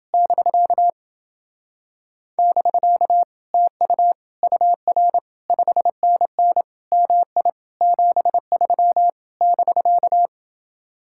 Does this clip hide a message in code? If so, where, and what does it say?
Yes: Morse code from 0.1 to 10.3 s, heard as a keyed tone, about 29 wpm, 700 Hz. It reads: <BK> <BK> TU UR 5NN MI 73 <BK>